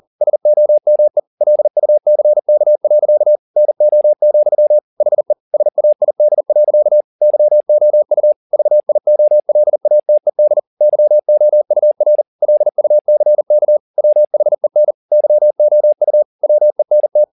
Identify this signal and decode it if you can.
Morse code, heard as a keyed tone, about 40 wpm, 600 Hz. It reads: SOME LUCK. NO, HE SAID. YOU VIOLATED YOUR LUCK WHEN YOU WENT
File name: the old man and the sea 2 40wpm